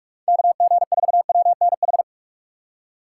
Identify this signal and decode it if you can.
Morse code, heard as a keyed tone, about 56 wpm, 700 Hz. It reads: KG4WNH
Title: Speed Racing - USA 2x3 Call Signs 45wpm